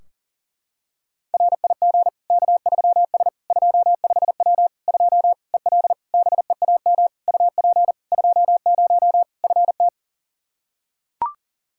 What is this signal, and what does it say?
Morse code, heard as a keyed tone, about 37 wpm, 700 Hz. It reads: RIG K3S 25W 2 EL BEAM UP 20 FT
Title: Practice QSO - Exchange 2 - Info - 40wpm